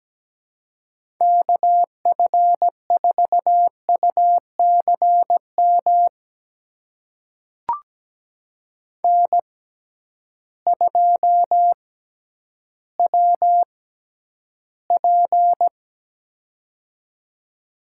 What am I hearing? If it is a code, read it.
Morse code, heard as a keyed tone, about 17 wpm, spaced as for 10 wpm, 700 Hz. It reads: KF4UCM N2WP
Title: USA Call Signs - 6x Character Spacing - 17wpm